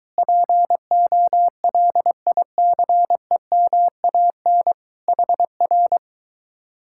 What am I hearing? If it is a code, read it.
Morse code, heard as a keyed tone, about 23 wpm, 700 Hz. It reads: POLICEMAN HR